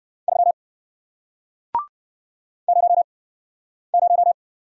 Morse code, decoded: 3 0 0

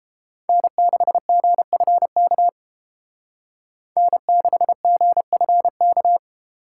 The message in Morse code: N6GFK N6GFK

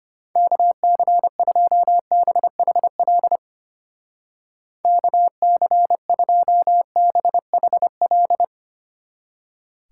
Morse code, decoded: KC2BHL KC2BHL

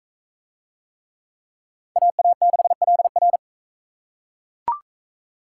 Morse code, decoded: AA6LR

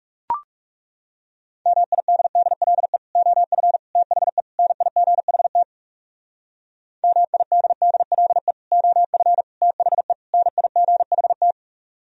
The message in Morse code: MIDDLE OF THE NIGHT MIDDLE OF THE NIGHT

